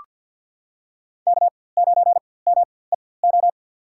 Morse code, heard as a keyed tone, about 47 wpm, spaced as for 23 wpm, 700 Hz. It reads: K9MEO
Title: USA Call Signs - 4x Character Spacing - 50wpm